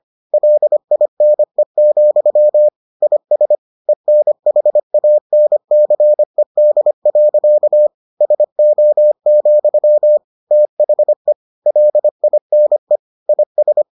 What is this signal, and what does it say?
Morse code, heard as a keyed tone, about 25 wpm, 600 Hz. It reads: LINE, IS ENHANCED. SO, THE LINE IS